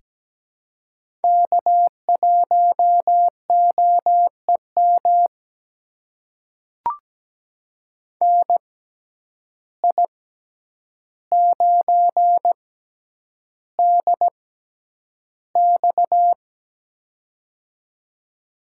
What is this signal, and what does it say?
Morse code, heard as a keyed tone, about 17 wpm, spaced as for 9 wpm, 700 Hz. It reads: K1OEM NI9DX
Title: USA Call Signs - 5 Characters - 6x Character Spacing - 17wpm